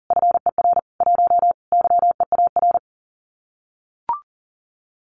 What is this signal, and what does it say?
Morse code, heard as a keyed tone, about 43 wpm, 700 Hz. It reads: FER 1 YEAR
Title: Common QSO Phrases Encore 40wpm